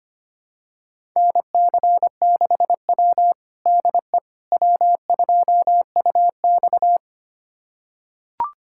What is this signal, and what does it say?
Morse code, heard as a keyed tone, about 25 wpm, 700 Hz. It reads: NC6W DE W2UX
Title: USA Callsign DE Callsign - 4 Characters - 25wpm